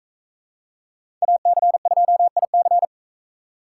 Morse code, heard as a keyed tone, about 41 wpm, 700 Hz. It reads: AC2IC